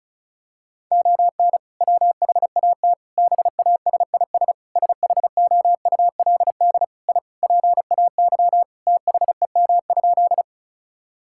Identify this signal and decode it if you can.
Morse code, heard as a keyed tone, about 36 wpm, 700 Hz. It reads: ON WHAT BASIS SHOULD I PAY THEM?
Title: Sentences from Top 1000 Words - Review Part 5 - 35wpm